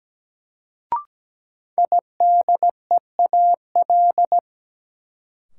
Morse code, heard as a keyed tone, about 17 wpm, 700 Hz. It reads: IDEAL
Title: Sentences from Top 1400 Words - New Word Drill - 17wpm